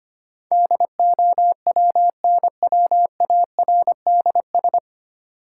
Morse code, heard as a keyed tone, about 25 wpm, 700 Hz. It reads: DOWNWARDS
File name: Nine Letter Words - 25wpm